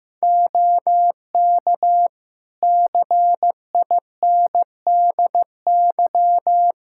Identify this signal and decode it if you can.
Morse code, heard as a keyed tone, about 15 wpm, 700 Hz. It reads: OK CINDY
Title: K1USN SST QSO Practice 15wpm